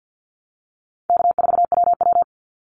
Morse code, heard as a keyed tone, about 50 wpm, 700 Hz. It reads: K4WW
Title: Easy Call Signs - 50wpm